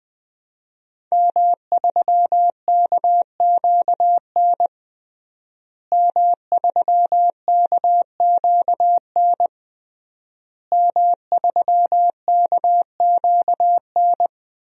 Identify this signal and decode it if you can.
Morse code, heard as a keyed tone, about 20 wpm, 700 Hz. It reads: M3KQN M3KQN M3KQN
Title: DXCC-100 Call Signs - 5 Characters - Repeated 3x - 20wpm